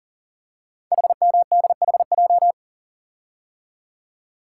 Morse code, heard as a keyed tone, about 38 wpm, 700 Hz. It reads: HMDHJ
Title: Code Groups - 5 Letters - 40wpm